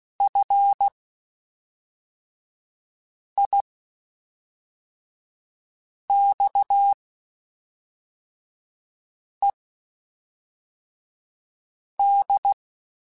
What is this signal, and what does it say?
Morse code, heard as a keyed tone, about 16 wpm, spaced as for 3 wpm, 800 Hz. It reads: FIXED
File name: (spaced) / QSOs Spaced-mp3s / WA7QOX 1 @5 sp